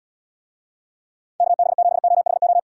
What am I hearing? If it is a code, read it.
Morse code, heard as a keyed tone, about 78 wpm, 700 Hz. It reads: KB7OHO